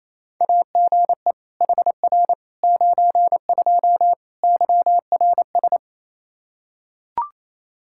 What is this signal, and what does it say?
Morse code, heard as a keyed tone, about 28 wpm, 700 Hz. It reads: AGE HR 92 YRS
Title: K1USN SST QSO Practice 28wpm